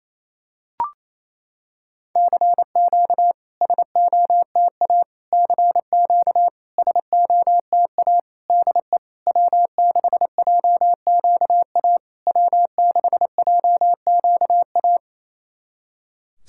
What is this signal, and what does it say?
Morse code, heard as a keyed tone, about 28 wpm, 700 Hz. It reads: CQ SOTA CQ SOTA DE W6JQA W6JQA
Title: SOTA QSO Practice 28wpm